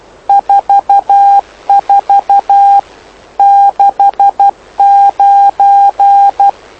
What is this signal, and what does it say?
Morse code, heard as a keyed tone, about 12 wpm, 800 Hz. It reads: 44 69
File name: lu4aao / lu7aa 09dic2011 exp planeador vuelo3